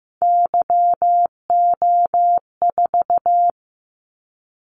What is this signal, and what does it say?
Morse code, heard as a keyed tone, about 15 wpm, 700 Hz. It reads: YO4
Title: Callsign Prefixes for Top 25 DXCC - 15wpm